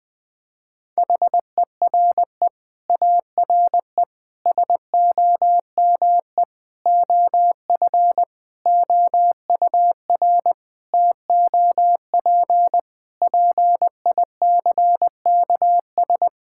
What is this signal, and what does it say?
Morse code, heard as a keyed tone, about 20 wpm, 700 Hz. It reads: HERE ARE SOME OF OUR TOP PICKS
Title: Sets of 9 Words - 20wpm